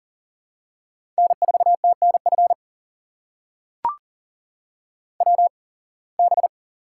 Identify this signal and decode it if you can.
Morse code, heard as a keyed tone, about 40 wpm, spaced as for 26 wpm, 700 Hz. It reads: N4TNF WB
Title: USA Call Signs - 8x Character Spacing - 40wpm